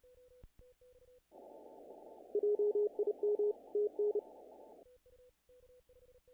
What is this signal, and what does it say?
Morse code, heard as a keyed tone, about 30 wpm, 420 Hz. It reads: JIM TN
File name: contests / naqp-cw-k5tr-op-k5pi- -1800